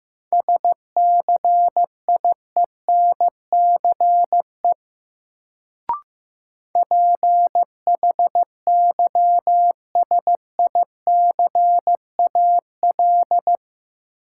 Morse code, heard as a keyed tone, about 15 wpm, 700 Hz. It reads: SCIENCE PHYSICAL
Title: Sentences from Top 1400 Words - New Word Drill Alt - 15wpm